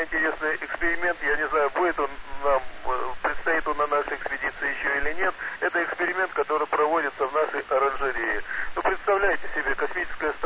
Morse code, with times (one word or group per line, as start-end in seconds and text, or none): none